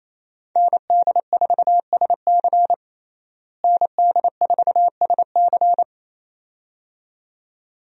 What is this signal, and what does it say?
Morse code, heard as a keyed tone, about 28 wpm, 700 Hz. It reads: ND4SC ND4SC
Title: USA Call Signs - 5 Characters - Repeated 2x - 28wpm